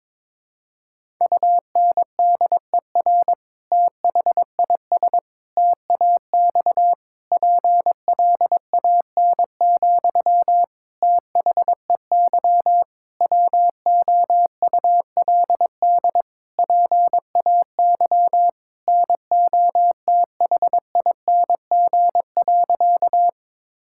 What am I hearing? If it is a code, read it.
Morse code, heard as a keyed tone, about 22 wpm, 700 Hz. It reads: UNDER THIS TAX PLAN, THEY WOULD PAY NOTHING.
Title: Sentences from Top 1000 Words - Review Part 5 - 22wpm